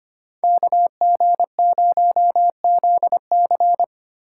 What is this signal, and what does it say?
Morse code, heard as a keyed tone, about 25 wpm, 700 Hz. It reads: KG0ZC
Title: USA Call Signs - 5 Characters - 4x Character Spacing - 25wpm